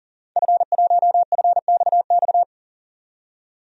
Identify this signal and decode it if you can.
Morse code, heard as a keyed tone, about 40 wpm, 700 Hz. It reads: F1FXX
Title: DXCC-100 Call Signs - 5 Characters - 40wpm